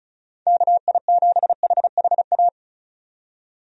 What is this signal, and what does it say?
Morse code, heard as a keyed tone, about 36 wpm, 700 Hz. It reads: KI7HHA
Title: USA Call Signs - 6 Characters - 35wpm